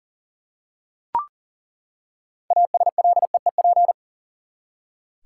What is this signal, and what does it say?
Morse code, heard as a keyed tone, about 40 wpm, 700 Hz. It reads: ASLEEP
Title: Six Letter Words - 40wpm